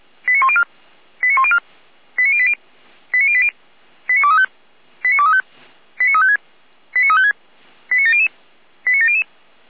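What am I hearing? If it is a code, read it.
Five-tone selective call (ZVEI), heard as quick runs of five tones, tones of about 80 ms. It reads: 89104 89104 89080 89080 89235 89235 89356 89356 89700 89700
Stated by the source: ZVEI five-tone call received by SDR.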